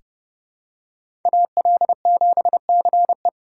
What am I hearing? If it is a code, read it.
Morse code, heard as a keyed tone, about 30 wpm, 700 Hz. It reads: AL7CE
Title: USA Call Signs - 4x Character Spacing - 30wpm